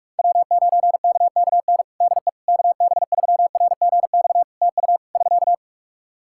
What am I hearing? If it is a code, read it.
Morse code, heard as a keyed tone, about 46 wpm, 700 Hz. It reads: W9KKN DE KB3RGX TU <SK>
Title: Practice QSO - Exchange 4 - Ending - 45wpm